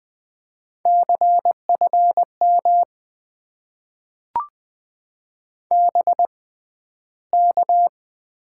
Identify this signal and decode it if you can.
Morse code, heard as a keyed tone, about 20 wpm, spaced as for 12 wpm, 700 Hz. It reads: CFM BK
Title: Top 90 QSO Elements - 6x Character Spacing - 20wpm